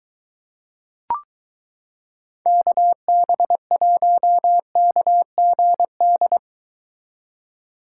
Morse code, heard as a keyed tone, about 23 wpm, 700 Hz. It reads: KB1KGD